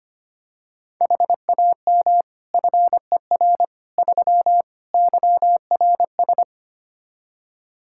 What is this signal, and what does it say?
Morse code, heard as a keyed tone, about 25 wpm, 700 Hz. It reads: HAM FER 3 YRS
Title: Practice QSO - Exchange 3 - More Info - 25wpm